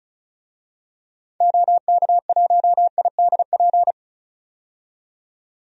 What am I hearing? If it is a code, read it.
Morse code, heard as a keyed tone, about 36 wpm, 700 Hz. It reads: OK1IDP
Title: DXCC-100 Call Signs - 6 Characters - 35wpm